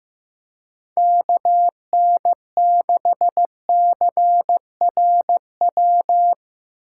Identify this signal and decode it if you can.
Morse code, heard as a keyed tone, about 15 wpm, 700 Hz. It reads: KN6CRW
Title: USA Call Signs - 6 Characters - 6x Character Spacing - 15wpm